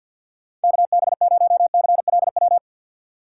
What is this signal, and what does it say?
Morse code, heard as a keyed tone, about 53 wpm, 700 Hz. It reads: K B 0 C L W
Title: USA Call Signs - 6 Characters - 8x Character Spacing - 50wpm